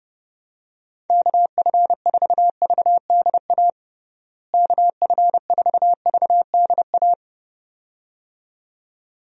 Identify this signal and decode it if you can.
Morse code, heard as a keyed tone, about 32 wpm, 700 Hz. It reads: KF4VDA KF4VDA
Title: USA Call Signs - 6 Characters - Repeated 2x - 30wpm